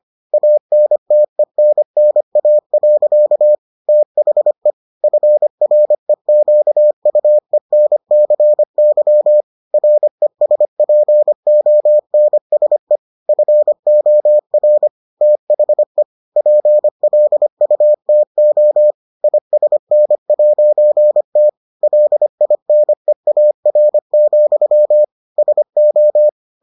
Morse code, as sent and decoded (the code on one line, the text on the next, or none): .- -. - . -. -. .- .-.-.- / - .... / ..-. .-. --.- ..- . -. -.-. -.-- / .-. ... .--. --- -. ... / ..-. --- .-. / - .... / .--. .-.. ..- - --- / .. ... -. .----. - / .-.. .. -. .- .-. --..-- / ... ---
ANTENNA. THE FREQUENCY RESPONSE FOR THE PLUTO ISN'T LINEAR, SO